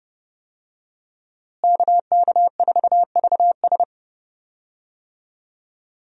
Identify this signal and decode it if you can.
Morse code, heard as a keyed tone, about 32 wpm, 700 Hz. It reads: KK4VS
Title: USA Call Signs - 5 Characters - Repeated 3x - 30wpm